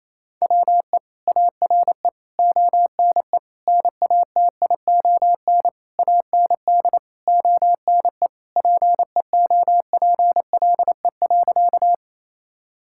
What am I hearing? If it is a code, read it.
Morse code, heard as a keyed tone, about 28 wpm, 700 Hz. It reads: WE ARE ONE NATION AND ONE PEOPLE.
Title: Sentences from Top 1000 Words - Review Part 6 - 28wpm